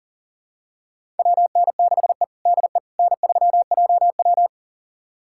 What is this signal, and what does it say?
Morse code, heard as a keyed tone, about 40 wpm, 700 Hz. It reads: WN6E DE N3JW
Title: USA Callsign DE Callsign - 4 Characters - 40wpm